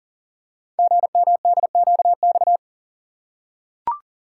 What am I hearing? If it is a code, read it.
Morse code, heard as a keyed tone, about 42 wpm, 700 Hz. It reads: GMDQX
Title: Code Groups - 5 Letters - 40wpm